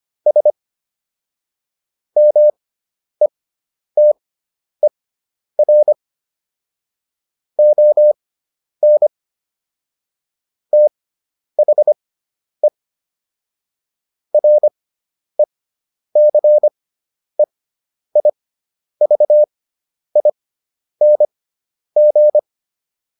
Morse code, heard as a keyed tone, about 25 wpm, spaced as for 10 wpm, 600 Hz. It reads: S METER ON THE RECEIVING